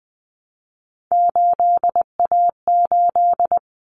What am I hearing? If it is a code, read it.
Morse code, heard as a keyed tone, about 20 wpm, 700 Hz. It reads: 8A8
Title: Common Cut Numbers - 3-Digit 20wpm